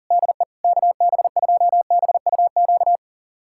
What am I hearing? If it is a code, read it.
Morse code, heard as a keyed tone, about 37 wpm, 700 Hz. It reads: DE KB2BUQ
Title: USA Callsign DE Callsign - 6 Characters - 40wpm